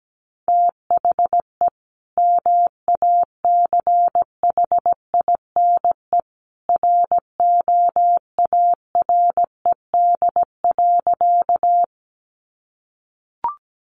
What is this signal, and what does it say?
Morse code, heard as a keyed tone, about 17 wpm, 700 Hz. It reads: THE MACHINE ROARED.
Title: A Sound Of Thunder 17wpm